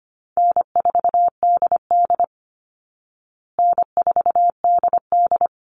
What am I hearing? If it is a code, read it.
Morse code, heard as a keyed tone, about 25 wpm, 700 Hz. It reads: N4DD N4DD
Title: Easy Call Signs Repeated 2x - 25wpm